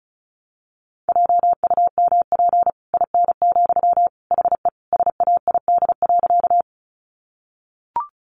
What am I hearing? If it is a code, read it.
Morse code, heard as a keyed tone, about 34 wpm, 700 Hz. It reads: JUMP IN, HE SAID.
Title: Sentences from Top 1300 Words 35wpm